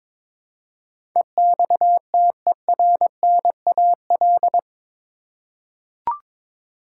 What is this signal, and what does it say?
Morse code, heard as a keyed tone, about 22 wpm, 700 Hz. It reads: EXTERNAL
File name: practice / 22wpm / Eight Letter Words - 22wpm